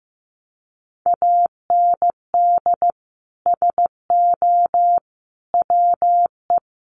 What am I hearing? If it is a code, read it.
Morse code, heard as a keyed tone, about 15 wpm, 700 Hz. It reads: AND SO WE
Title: Sentences from Top 1500 Words 15wpm